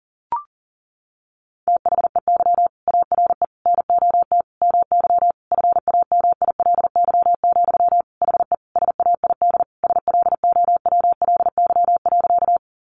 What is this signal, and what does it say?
Morse code, heard as a keyed tone, about 40 wpm, 700 Hz. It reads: THEY ARE NOT MY FAMILY, HE SAID SLOWLY.
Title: Sentences from Top 1100 Words 40wpm